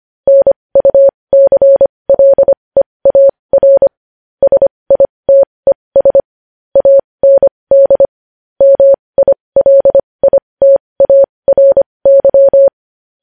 Morse code, heard as a keyed tone, about 25 wpm, spaced as for 20 wpm, 550 Hz. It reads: NUCLEAR SITES AND MILITARY
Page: news-headlines-in-morse-code-at-20-wpm